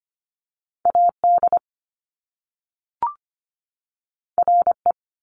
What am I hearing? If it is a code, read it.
Morse code, heard as a keyed tone, about 25 wpm, 700 Hz. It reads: AD RE
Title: Top Two Letter Combinations in Words 25wpm